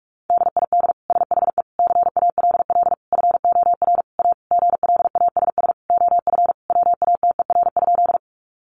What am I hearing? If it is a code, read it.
Morse code, heard as a keyed tone, about 44 wpm, 700 Hz. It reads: DID SHE CALL FOR A GLASS OF WATER?
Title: Sentences from Top 1300 Words 45wpm